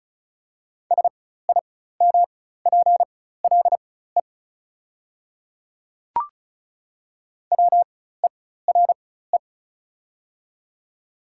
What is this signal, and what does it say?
Morse code, heard as a keyed tone, about 36 wpm, spaced as for 16 wpm, 700 Hz. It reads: SIMPLE WERE
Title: 4x Character Spacing - Top 500 Words - 35wpm